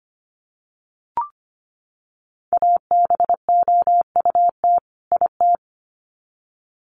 Morse code, heard as a keyed tone, about 25 wpm, 700 Hz. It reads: ABOUT IT
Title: Sets of 2 Words 25wpm